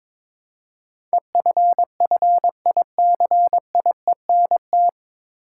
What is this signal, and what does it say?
Morse code, heard as a keyed tone, about 22 wpm, 700 Hz. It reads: EFFICIENT
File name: 9 Letter Words - First 3 Letters Given - 22wpm